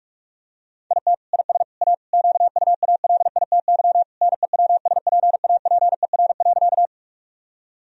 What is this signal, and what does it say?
Morse code, heard as a keyed tone, about 47 wpm, 700 Hz. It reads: IT IS A QUALITY NEWSPAPER.
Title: Sentences from Top 1000 Words - Review Part 4 - 45wpm